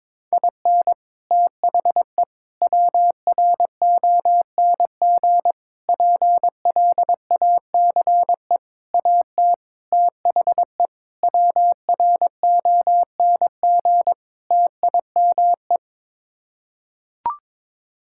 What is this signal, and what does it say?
Morse code, heard as a keyed tone, about 22 wpm, 700 Hz. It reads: IN THE WRONG PLACE AT THE WRONG TIME
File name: practice / 22wpm / Sets of 8 Words - 22wpm